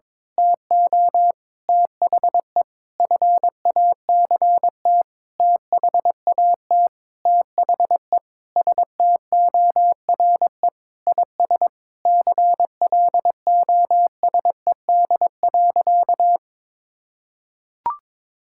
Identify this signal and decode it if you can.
Morse code, heard as a keyed tone, about 22 wpm, 700 Hz. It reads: TO THE FACT THAT THE STORE IS CLOSED.
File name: Sentences with 7-Word Phrases - 22wpm